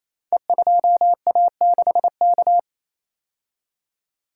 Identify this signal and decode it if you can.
Morse code, heard as a keyed tone, about 29 wpm, 700 Hz. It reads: E2A6K
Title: Code Groups - 5 Letter:Numbers - 28wpm